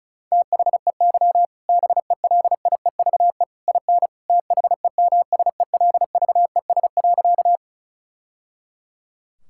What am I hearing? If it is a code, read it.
Morse code, heard as a keyed tone, about 34 wpm, 700 Hz. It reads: THEY BELIEVE IN THEMSELVES.
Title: Sentences from Top 1000 Words - Review Part 5 - 35wpm